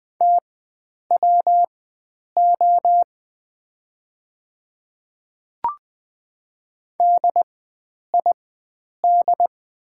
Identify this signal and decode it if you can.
Morse code, heard as a keyed tone, about 20 wpm, spaced as for 9 wpm, 700 Hz. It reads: TWO DID
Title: 4x Character Spacing - Top 300 Words - 20wpm